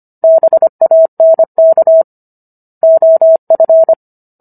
Morse code, heard as a keyed tone, about 25 wpm, 650 Hz. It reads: BANK OF